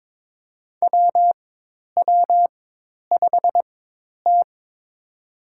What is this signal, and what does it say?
Morse code, heard as a keyed tone, about 22 wpm, spaced as for 10 wpm, 700 Hz. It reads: WW5T
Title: USA Call Signs - 4 Characters - 4x Character Spacing - 22wpm